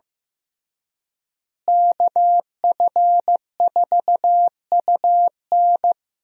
Morse code, heard as a keyed tone, about 15 wpm, 700 Hz. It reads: KF4UN
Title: USA Call Signs - 5 Characters - Repeated 3x - 15wpm